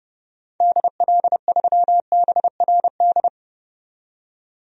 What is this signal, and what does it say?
Morse code, heard as a keyed tone, about 30 wpm, 690 Hz. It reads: DL3BRD
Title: DXCC-100 Call Signs - 6 Characters - 30wpm